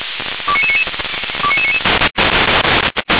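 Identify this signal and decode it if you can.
Five-tone selective call (ZVEI), heard as quick runs of five tones, tones of about 70 ms. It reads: 29999 29999
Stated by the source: ZVEI five-tone selective call, code 29999, received via SDR.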